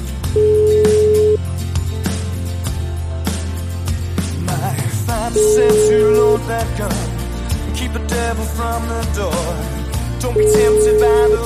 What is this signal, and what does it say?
Ringback tone (Europe, 425 Hz), with noise.